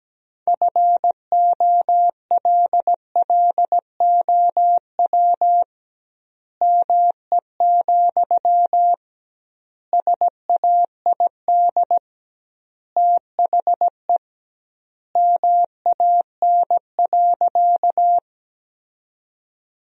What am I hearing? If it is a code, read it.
Morse code, heard as a keyed tone, about 17 wpm, 700 Hz. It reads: FOLLOW ME, SAID THE MAN.
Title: 2x Word-Spacing - Sentences from Top 200 Words - 17wpm